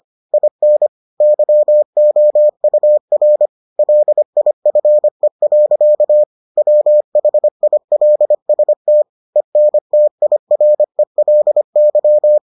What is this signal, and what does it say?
Morse code, heard as a keyed tone, about 25 wpm, 600 Hz. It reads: IN YOUR LIFE. WHILST ENTIRELY